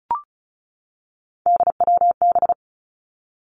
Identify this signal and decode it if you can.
Morse code, heard as a keyed tone, about 36 wpm, 700 Hz. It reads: DWB
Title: Three Challenging Letter-Number Combinations 35wpm